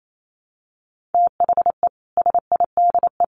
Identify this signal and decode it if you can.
Morse code, heard as a keyed tone, about 28 wpm, 700 Hz. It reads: THE SIDE
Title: Sets of 3 Words 28wpm